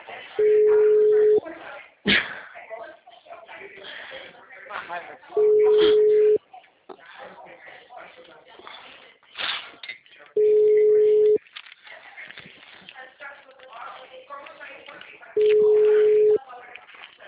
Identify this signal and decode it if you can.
Ringback tone (Europe, 425 Hz), with speech.